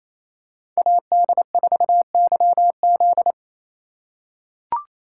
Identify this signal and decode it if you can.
Morse code, heard as a keyed tone, about 28 wpm, 700 Hz. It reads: AD4YZ
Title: USA Call Signs - 5 Characters - 2x Character Spacing - 28wpm